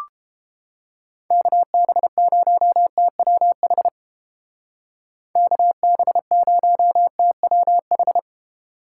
Morse code, heard as a keyed tone, about 29 wpm, 700 Hz. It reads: KB0TWH KB0TWH